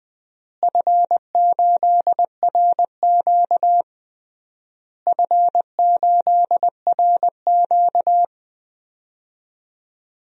Morse code, heard as a keyed tone, about 21 wpm, 700 Hz. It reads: F8RQ F8RQ
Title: DXCC-100 Call Signs - 4 Characters - Repeated 3x - 20wpm